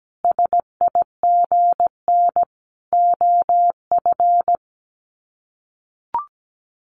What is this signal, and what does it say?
Morse code, heard as a keyed tone, about 17 wpm, 700 Hz. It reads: SIGN OF
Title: Sets of 5 Words Encore2 17wpm